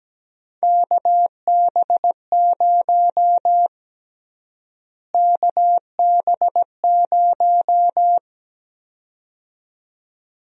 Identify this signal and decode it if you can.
Morse code, heard as a keyed tone, about 17 wpm, 700 Hz. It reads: KB0 KB0
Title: USA Call Sign Prefixes - Repeated 2x - 17wpm